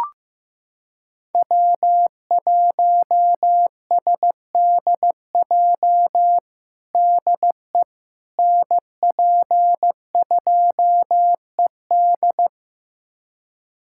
Morse code, heard as a keyed tone, about 15 wpm, 700 Hz. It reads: W1SDJ DE NP2ED
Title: USA Callsign DE Callsign - 5 Characters - 15wpm